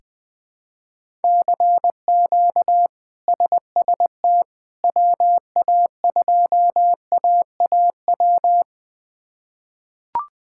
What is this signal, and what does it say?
Morse code, heard as a keyed tone, about 20 wpm, 700 Hz. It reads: CQ SST WA2AAW
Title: K1USN SST QSO Practice - 20wpm